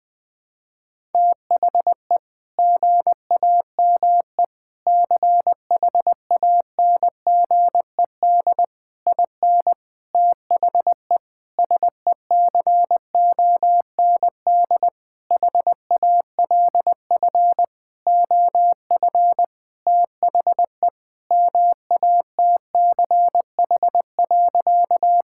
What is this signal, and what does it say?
Morse code, heard as a keyed tone, about 20 wpm, 700 Hz. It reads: THE GAME CHANGED IN THE SECOND HALF OF THE MATCH.
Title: Sentences with 5-Word Phrases - 20wpm